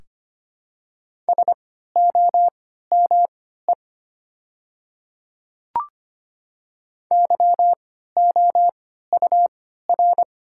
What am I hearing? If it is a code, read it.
Morse code, heard as a keyed tone, about 25 wpm, spaced as for 14 wpm, 700 Hz. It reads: SOME YOUR
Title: 3x Character Spacing - Top 300 Words - 25wpm